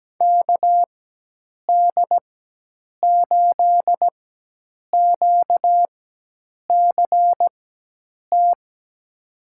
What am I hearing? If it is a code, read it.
Morse code, heard as a keyed tone, about 17 wpm, spaced as for 8 wpm, 700 Hz. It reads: KD8QCT
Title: USA Call Signs - 4x Character Spacing - 17wpm